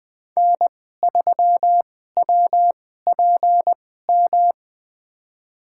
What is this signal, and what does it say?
Morse code, heard as a keyed tone, about 20 wpm, spaced as for 15 wpm, 700 Hz. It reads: N3WPM
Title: USA Call Signs - 5 Characters - 2x Character Spacing - 20wpm